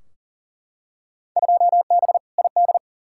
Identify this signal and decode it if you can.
Morse code, heard as a keyed tone, about 39 wpm, 700 Hz. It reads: 2B ID
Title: Field Day Exchanges Encore - 40wpm